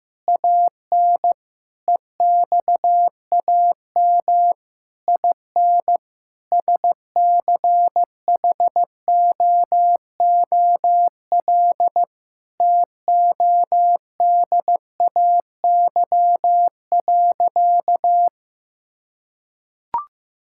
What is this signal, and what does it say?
Morse code, heard as a keyed tone, about 15 wpm, 700 Hz. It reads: AN EXAM IN SCHOOL TODAY.
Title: Sentences from Top 1000 Words - Review Part 5 - 15wpm